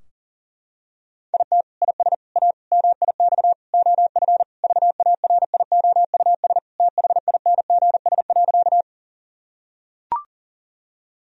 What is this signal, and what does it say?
Morse code, heard as a keyed tone, about 40 wpm, 700 Hz. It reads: IT IS A MIX OF VARIOUS THINGS.
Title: Sentences from Top 1000 Words - Review Part 4 - 40wpm